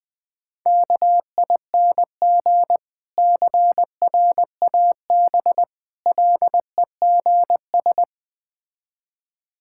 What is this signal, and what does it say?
Morse code, heard as a keyed tone, about 20 wpm, 700 Hz. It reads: KING CRAB LEGS
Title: Difficult Words with Hints - 20wpm